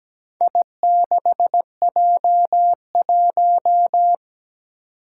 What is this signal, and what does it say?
Morse code, heard as a keyed tone, about 17 wpm, 700 Hz. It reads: I6J1
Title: Four Challenging Letter-Number Combinations 17wpm